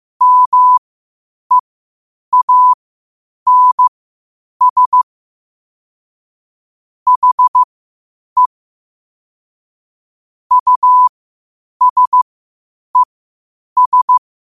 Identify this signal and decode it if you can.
Morse code, heard as a keyed tone, about 15 wpm, spaced as for 8 wpm, 1000 Hz. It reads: MEANS HE USES
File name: qotdfiles / 8wpm